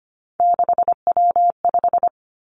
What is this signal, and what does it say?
Morse code, heard as a keyed tone, about 25 wpm, 700 Hz. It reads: 6W5
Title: Three Challenging Letter-Number Combinations 25wpm